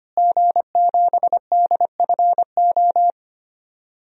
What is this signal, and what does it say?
Morse code, heard as a keyed tone, about 25 wpm, 700 Hz. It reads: G7DFO
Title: DXCC-100 Call Signs - 5 Characters - Repeated 2x - 25wpm